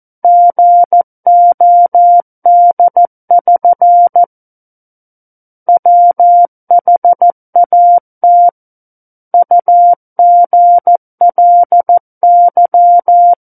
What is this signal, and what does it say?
Morse code, heard as a keyed tone, about 14 wpm, 700 Hz. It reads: GOD<SN> WHAT UGLY
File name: Book One / 05-Chapter 5 - 14 WPM